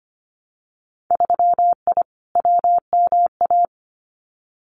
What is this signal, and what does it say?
Morse code, heard as a keyed tone, about 25 wpm, 700 Hz. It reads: 3I WMA
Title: Winter Field Day Exchanges Encore 25wpm